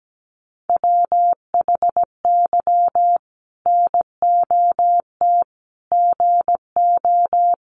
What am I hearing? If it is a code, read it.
Morse code, heard as a keyed tone, about 17 wpm, 700 Hz. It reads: WHY NOT GO
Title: Sentences from Top 1400 Words 17wpm